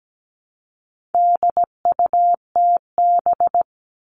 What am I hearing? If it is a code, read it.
Morse code, heard as a keyed tone, about 17 wpm, 700 Hz. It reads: DUTB
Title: All Cut Numbers - 4-Digit 17wpm